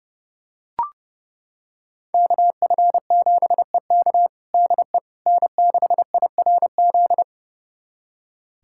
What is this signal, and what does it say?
Morse code, heard as a keyed tone, about 30 wpm, 700 Hz. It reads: KF7EK DE N6IRZ
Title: USA Callsign DE Callsign - 5 Characters - 30wpm